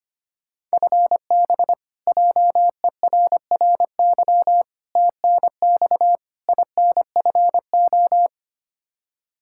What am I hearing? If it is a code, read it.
Morse code, heard as a keyed tone, about 25 wpm, 700 Hz. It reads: FB JERRY TNX INFO
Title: K1USN SST QSO Practice 25wpm